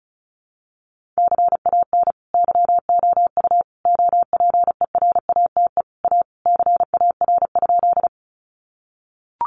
Morse code, heard as a keyed tone, about 34 wpm, 700 Hz. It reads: CAN YOU OPERATE A CAR?
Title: Sentences from Top 1500 Words 35wpm